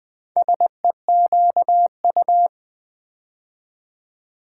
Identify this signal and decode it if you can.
Morse code, heard as a keyed tone, about 20 wpm, 700 Hz. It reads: SEQU
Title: Build A Word - 2 Letters At A Time - 8-Letter Words - 20wpm